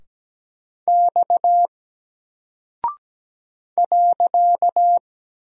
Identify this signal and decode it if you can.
Morse code, heard as a keyed tone, about 17 wpm, 700 Hz. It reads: X .
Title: Single Character (Letter:Number:Punctuation:Prosign) Alt 17wpm